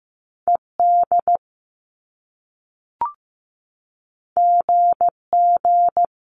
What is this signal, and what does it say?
Morse code, heard as a keyed tone, about 15 wpm, 700 Hz. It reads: ED GG
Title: Top Two Letter Combinations in Words 15wpm